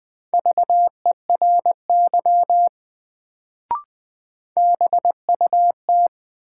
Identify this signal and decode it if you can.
Morse code, heard as a keyed tone, about 20 wpm, 700 Hz. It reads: VERY BUT
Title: Sets of 2 Words - New Word Drill - 20wpm